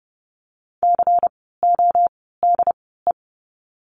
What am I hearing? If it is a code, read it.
Morse code, heard as a keyed tone, about 30 wpm, spaced as for 17 wpm, 700 Hz. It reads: CODE
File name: Lesson 16f - Letter c - Words 30wpm